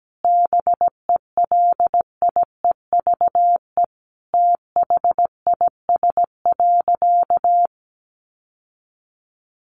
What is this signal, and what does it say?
Morse code, heard as a keyed tone, about 17 wpm, 700 Hz. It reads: BELIEVE THIS.
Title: Sentences from Top 1300 Words 17wpm